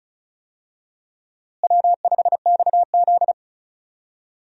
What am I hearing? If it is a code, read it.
Morse code, heard as a keyed tone, about 36 wpm, 700 Hz. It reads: W5XZ
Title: USA Call Signs - 4 Characters - Repeated 2x - 35wpm